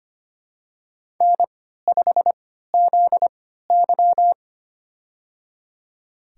Morse code, heard as a keyed tone, about 25 wpm, spaced as for 14 wpm, 700 Hz. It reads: N5ZY
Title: USA Call Signs - 4 Characters - 3x Character Spacing - 25wpm